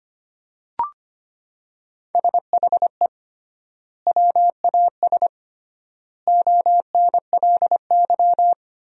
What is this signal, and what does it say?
Morse code, heard as a keyed tone, about 25 wpm, 700 Hz. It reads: SHE WAS ONLY
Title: 3x Word-Spacing - Sentences from Top 400 Words - 25wpm